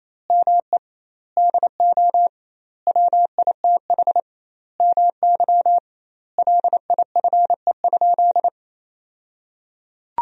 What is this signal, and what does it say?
Morse code, heard as a keyed tone, about 28 wpm, 700 Hz. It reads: ME DO WITH MY LIFE?
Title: 2x Word-Spacing - Sentences from Top 300 Words - 28wpm